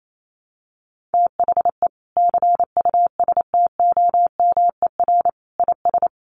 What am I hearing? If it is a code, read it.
Morse code, heard as a keyed tone, about 28 wpm, 700 Hz. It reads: THE CUSTOMER IS
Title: Sentences from Top 1100 Words 28wpm